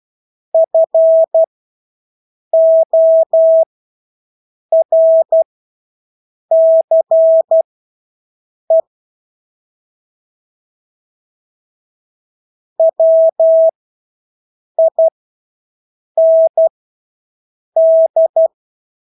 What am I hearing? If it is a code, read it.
Morse code, heard as a keyed tone, about 12 wpm, spaced as for 6 wpm, 650 Hz. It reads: FORCE WIND
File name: Engelske ord 12-6 wpm